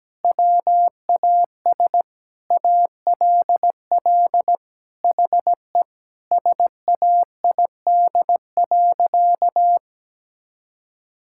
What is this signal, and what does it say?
Morse code, heard as a keyed tone, about 17 wpm, 700 Hz. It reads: WAS ALL HE SAID.
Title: Sentences from Top 1600 Words 17wpm